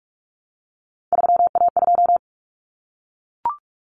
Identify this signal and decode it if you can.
Morse code, heard as a keyed tone, about 44 wpm, 700 Hz. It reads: 3A2
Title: Common Cut Numbers - 3-Digit 45wpm